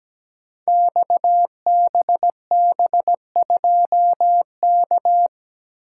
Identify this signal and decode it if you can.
Morse code, heard as a keyed tone, about 17 wpm, 700 Hz. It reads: XBB2K
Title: Code Groups - 5 Letter:Numbers - 17wpm